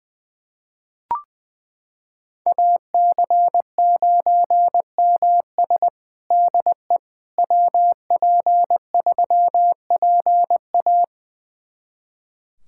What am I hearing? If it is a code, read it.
Morse code, heard as a keyed tone, about 20 wpm, 700 Hz. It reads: AC9MS DE WP3PA
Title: USA Callsign DE Callsign - 5 Characters - 20wpm